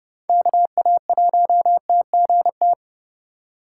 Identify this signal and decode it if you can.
Morse code, heard as a keyed tone, about 32 wpm, 700 Hz. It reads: KA1TGT
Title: USA Call Signs - 6 Characters - 4x Character Spacing - 30wpm